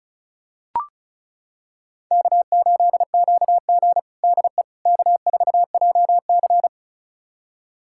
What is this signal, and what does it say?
Morse code, heard as a keyed tone, about 36 wpm, 700 Hz. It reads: K8QG DE K4JC
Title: USA Callsign DE Callsign - 4 Characters - 35wpm